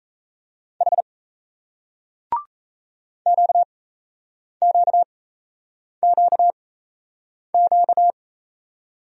Morse code, heard as a keyed tone, about 38 wpm, spaced as for 8 wpm, 700 Hz. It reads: H QQQQ